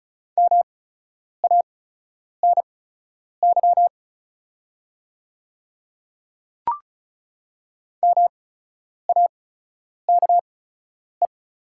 Morse code, heard as a keyed tone, about 36 wpm, spaced as for 10 wpm, 700 Hz. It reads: MANY MAKE
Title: 8x Character Spacing - Top 300 Words - 35wpm